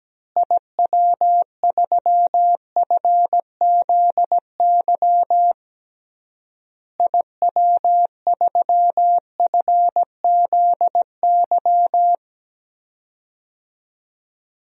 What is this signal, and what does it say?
Morse code, heard as a keyed tone, about 17 wpm, 700 Hz. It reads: IW3FZY IW3FZY
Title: DXCC-100 Call Signs - 6 Characters - Repeated 3x - 17wpm